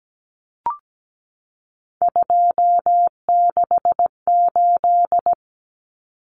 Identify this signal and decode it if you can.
Morse code, heard as a keyed tone, about 17 wpm, 700 Hz. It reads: 268
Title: Common Cut Numbers - 3-Digit 17wpm